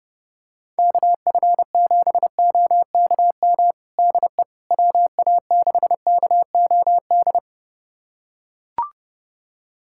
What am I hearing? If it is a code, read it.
Morse code, heard as a keyed tone, about 32 wpm, 690 Hz. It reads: KF7OKM DE WA6KOD
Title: USA Callsign DE Callsign - 6 Characters - 30wpm